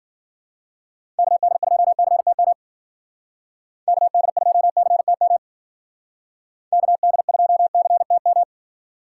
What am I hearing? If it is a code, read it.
Morse code, heard as a keyed tone, about 52 wpm, 700 Hz. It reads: KD2CTM KD2CTM KD2CTM